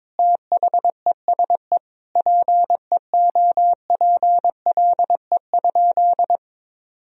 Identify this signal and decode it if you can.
Morse code, heard as a keyed tone, about 22 wpm, 700 Hz. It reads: THESE PEOPLE?
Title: Sentences from Top 1000 Words - Review Part 3 - 22wpm